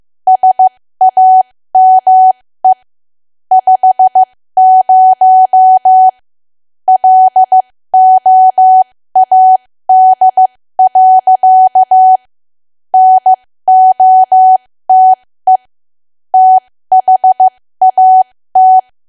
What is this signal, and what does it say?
Morse code, heard as a keyed tone, about 15 wpm, spaced as for 13 wpm, 750 Hz. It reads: SAME 50 LOAD. NOTE THAT
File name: cwa / W1AW-Level2 / 13WPM